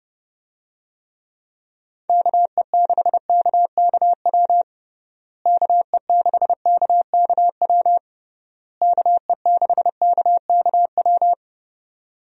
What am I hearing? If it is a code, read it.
Morse code, heard as a keyed tone, about 32 wpm, 700 Hz. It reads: KE6KKW KE6KKW KE6KKW
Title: USA Call Signs - 6 Characters - Repeated 3x - 30wpm